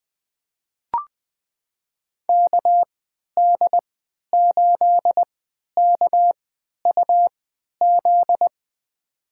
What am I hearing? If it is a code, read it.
Morse code, heard as a keyed tone, about 21 wpm, spaced as for 11 wpm, 700 Hz. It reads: KD8KUZ
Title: USA Call Signs - 6 Characters - 3x Character Spacing - 20wpm